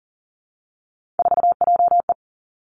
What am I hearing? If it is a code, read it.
Morse code, heard as a keyed tone, about 38 wpm, 700 Hz. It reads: 4JE
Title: Three Challenging Letter-Number Combinations 40wpm